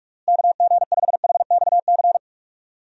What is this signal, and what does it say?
Morse code, heard as a keyed tone, about 46 wpm, 700 Hz. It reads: KG5HXC
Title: USA Call Signs - 6 Characters - Repeated 3x - 45wpm